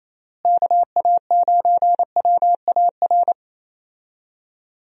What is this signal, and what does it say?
Morse code, heard as a keyed tone, about 27 wpm, 700 Hz. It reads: KA9WAR
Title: USA Call Signs - 6 Characters - Repeated 2x - 28wpm